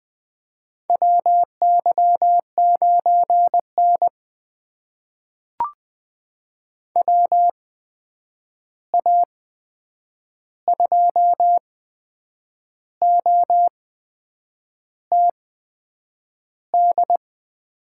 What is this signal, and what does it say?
Morse code, heard as a keyed tone, about 20 wpm, spaced as for 8 wpm, 700 Hz. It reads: WY9N WA2OTD
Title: USA Call Signs - 8x Character Spacing - 20wpm